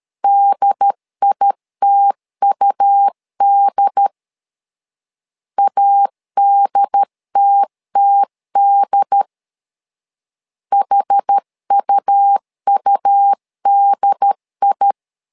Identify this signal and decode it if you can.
Morse code, heard as a keyed tone, about 13 wpm, 800 Hz. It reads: DITUD ADTTD HUUDI
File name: LEZIONI / 2 UHDN / 2 UHD 12wpm 1540